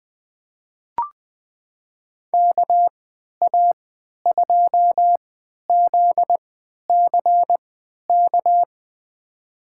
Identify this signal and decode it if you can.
Morse code, heard as a keyed tone, about 20 wpm, spaced as for 11 wpm, 700 Hz. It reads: KA2ZCK
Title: USA Call Signs - 6 Characters - 3x Character Spacing - 20wpm